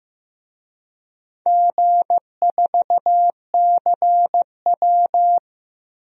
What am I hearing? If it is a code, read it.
Morse code, heard as a keyed tone, about 15 wpm, 700 Hz. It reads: G4CW
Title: DXCC-100 Call Signs - 4 Characters - 15wpm